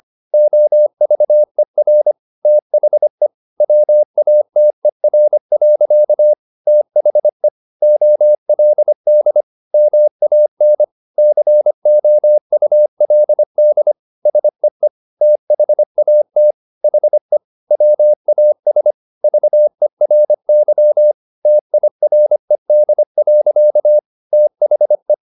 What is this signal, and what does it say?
Morse code, heard as a keyed tone, about 25 wpm, 600 Hz. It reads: OVER THE WATER. THE OLD MAN COULD SEE THAT HE WAS VERY TIRED. THE
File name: the old man and the sea part 1 25wpm